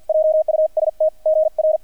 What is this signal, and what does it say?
Morse code, heard as a keyed tone, about 50 wpm, spaced as for 44 wpm, 630 Hz, with radio noise. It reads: JUST GR
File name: archived / CRSnet